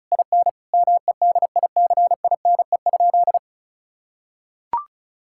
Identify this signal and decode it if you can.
Morse code, heard as a keyed tone, about 34 wpm, 700 Hz. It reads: IN MEDICINE?
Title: Sentences from Top 1000 Words - Review Part 6 - 35wpm